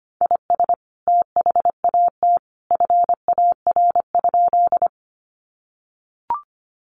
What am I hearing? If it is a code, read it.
Morse code, heard as a keyed tone, about 25 wpm, 700 Hz. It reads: IS THAT FAR?
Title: Sentences from Top 300 Words 25wpm